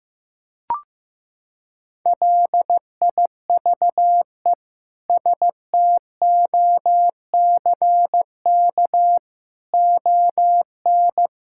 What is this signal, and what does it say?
Morse code, heard as a keyed tone, about 16 wpm, 700 Hz. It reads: LIVE STOCK ON
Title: Sentences from Top 1000 Words - Review Part 5 - 15wpm